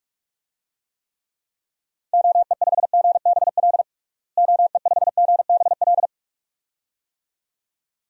Morse code, heard as a keyed tone, about 46 wpm, 700 Hz. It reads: OE5GBL OE5GBL
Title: DXCC-100 Call Signs - 6 Characters - Repeated 2x - 45wpm